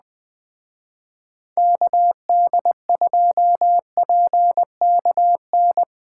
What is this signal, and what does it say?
Morse code, heard as a keyed tone, about 20 wpm, 700 Hz. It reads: KD2PKN